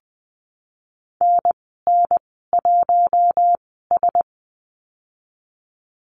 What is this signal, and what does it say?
Morse code, heard as a keyed tone, about 20 wpm, spaced as for 15 wpm, 700 Hz. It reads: NN1S
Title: 055 Lesson 10g - Letter r - Call Signs 20wpm